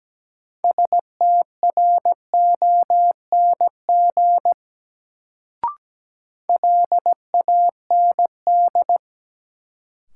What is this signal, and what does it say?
Morse code, heard as a keyed tone, about 17 wpm, 700 Hz. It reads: STRONG LAND
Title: Sentences from Top 500 Words - New Word Drill - 17wpm